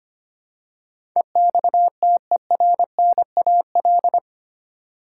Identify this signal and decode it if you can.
Morse code, heard as a keyed tone, about 25 wpm, 700 Hz. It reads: EXTERNAL
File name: 8 Letter Words - First 3 Letters Given - 25wpm